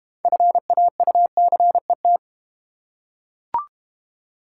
Morse code, heard as a keyed tone, about 30 wpm, 690 Hz. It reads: FAUCET
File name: Easy Words - Speed-Racing with Hints - 25wpm